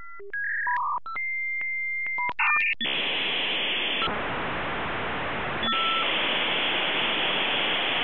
Dial-up modem handshake.
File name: Modem Sounds / unk unk v92